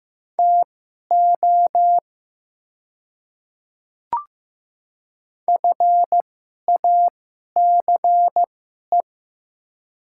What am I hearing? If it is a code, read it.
Morse code, heard as a keyed tone, about 16 wpm, spaced as for 11 wpm, 700 Hz. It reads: TO FACE
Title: 2x Character Spacing - Top 300 Words - 15wpm